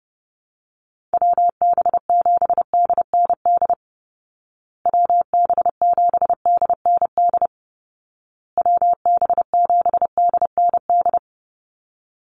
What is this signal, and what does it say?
Morse code, heard as a keyed tone, about 30 wpm, 700 Hz. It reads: WB7DND WB7DND WB7DND
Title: Call Signs Repeated 2x - 30wpm